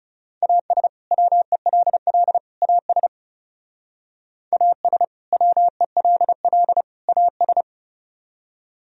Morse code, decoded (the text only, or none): AS WELL AS AS WELL AS